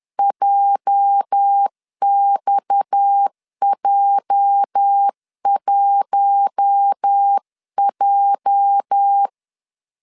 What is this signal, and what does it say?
Morse code, heard as a keyed tone, about 10 wpm, 800 Hz. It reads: JXJ1J
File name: LEZIONI / 8 YJ45 / 8 YJ 10wpm 2651